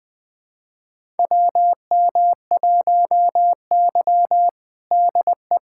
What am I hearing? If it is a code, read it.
Morse code, heard as a keyed tone, about 20 wpm, 700 Hz. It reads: WM1Y DE
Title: USA Callsign DE Callsign - 4 Characters - 20wpm